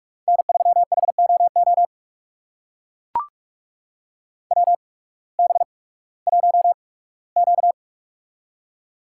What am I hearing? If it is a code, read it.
Morse code, heard as a keyed tone, about 44 wpm, spaced as for 21 wpm, 700 Hz. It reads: N3HOO WB1Q
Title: USA Call Signs - 8x Character Spacing - 45wpm